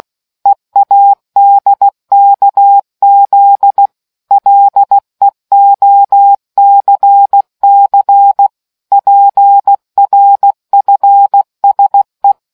Morse code, heard as a keyed tone, about 16 wpm, 800 Hz. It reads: EADKZ LEOCC PRFSE